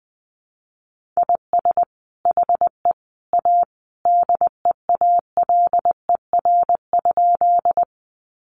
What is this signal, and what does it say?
Morse code, heard as a keyed tone, about 20 wpm, 700 Hz. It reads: IS HE A DEALER?
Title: Sentences from Top 1100 Words 20wpm